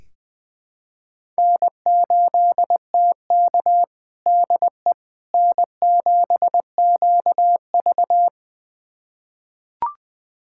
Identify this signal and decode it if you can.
Morse code, heard as a keyed tone, about 20 wpm, 700 Hz. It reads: N8TK DE N7QV
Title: USA Callsign DE Callsign - 4 Characters - 20wpm